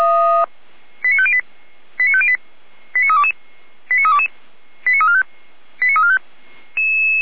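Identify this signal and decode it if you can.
Five-tone selective call (ZVEI), heard as quick runs of five tones, tones of about 70 ms. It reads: 89508 89508 89322 89322 89345 89345
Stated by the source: ZVEI five-tone selective call received on an SDR.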